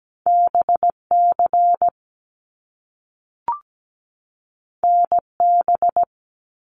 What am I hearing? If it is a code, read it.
Morse code, heard as a keyed tone, about 17 wpm, 700 Hz. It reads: BC NB
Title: Canadian Provinces and Territories 17wpm